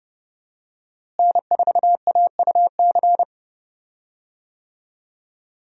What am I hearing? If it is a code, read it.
Morse code, heard as a keyed tone, about 30 wpm, 700 Hz. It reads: N4AUC
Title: USA Call Signs - 5 Characters - Repeated 2x - 30wpm